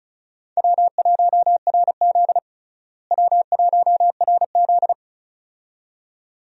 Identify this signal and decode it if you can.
Morse code, heard as a keyed tone, about 36 wpm, 700 Hz. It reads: W1RZ W1RZ
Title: USA Call Signs - 4 Characters - Repeated 2x - 35wpm